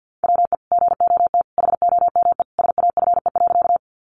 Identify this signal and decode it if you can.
Morse code, heard as a keyed tone, about 50 wpm, 700 Hz. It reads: WE GOT HOME SAFE.
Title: Sentences from Top 900 Words 50wpm